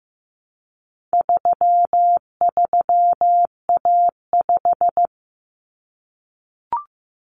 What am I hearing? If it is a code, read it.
Morse code, heard as a keyed tone, about 15 wpm, 700 Hz. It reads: 33A5
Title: Common Cut Numbers - 4-Digit 15wpm